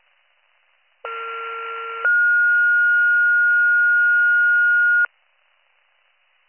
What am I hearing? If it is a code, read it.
Two-tone page, heard as a one-second tone, then a three-second tone, about 1370 then 1430 Hz. It is heard from 1.1 to 5.1 s.